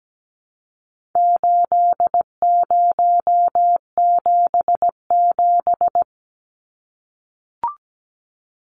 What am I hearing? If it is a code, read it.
Morse code, heard as a keyed tone, about 17 wpm, 700 Hz. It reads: 8077
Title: Numbers - 4-Digit 17wpm